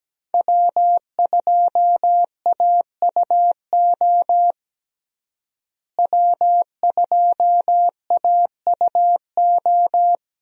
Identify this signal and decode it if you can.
Morse code, heard as a keyed tone, about 17 wpm, 700 Hz. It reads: W2AUO W2AUO
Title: USA Call Signs - 5 Characters - Repeated 3x - 17wpm